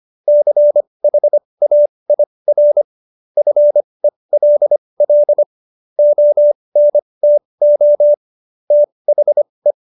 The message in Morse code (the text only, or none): CHAIR FELL ONTO THE